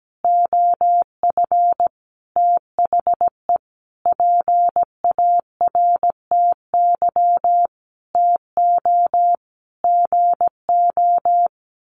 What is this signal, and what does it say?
Morse code, heard as a keyed tone, about 17 wpm, 700 Hz. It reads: OF THE PARTY TO GO
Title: Sentences from Top 1400 Words 17wpm